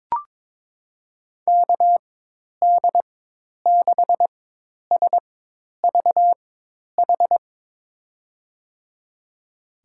Morse code, heard as a keyed tone, about 22 wpm, spaced as for 10 wpm, 700 Hz. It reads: KD6SVH
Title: USA Call Signs - 4x Character Spacing - 22wpm